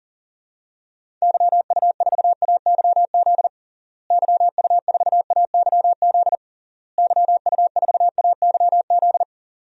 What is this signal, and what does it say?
Morse code, heard as a keyed tone, about 40 wpm, 700 Hz. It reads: YU4AYZ YU4AYZ YU4AYZ
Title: DXCC-100 Call Signs - 6 Characters - Repeated 3x - 40wpm